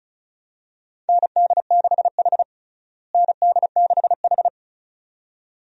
Morse code, heard as a keyed tone, about 36 wpm, 700 Hz. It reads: ND6H ND6H
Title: USA Call Signs - 4 Characters - Repeated 2x - 35wpm